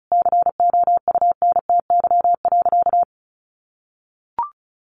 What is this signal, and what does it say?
Morse code, heard as a keyed tone, about 34 wpm, 700 Hz. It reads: COUNTY.
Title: Sentences from Top 1500 Words 35wpm